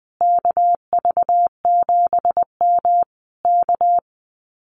Morse code, heard as a keyed tone, about 20 wpm, 700 Hz. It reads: KV7M K
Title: Common QSO Phrases Encore 20wpm